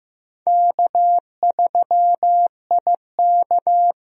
Morse code, heard as a keyed tone, about 16 wpm, 700 Hz. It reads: K3IK
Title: USA Call Signs - 4 Characters - 15wpm